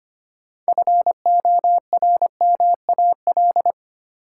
Morse code, heard as a keyed tone, about 25 wpm, 700 Hz. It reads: FORMAL
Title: Sentences from Top 1300 Words - New Word Drill - 25wpm